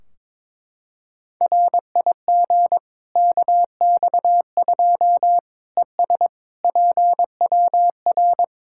Morse code, heard as a keyed tone, about 22 wpm, 700 Hz. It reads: RIG KX2 ES PWR
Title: Practice QSO - Exchange 2 - Info - 22wpm